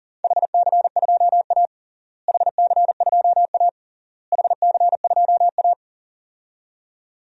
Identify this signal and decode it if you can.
Morse code, heard as a keyed tone, about 40 wpm, 700 Hz. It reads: HC2A HC2A HC2A
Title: DXCC-100 Call Signs - 4 Characters - Repeated 3x - 40wpm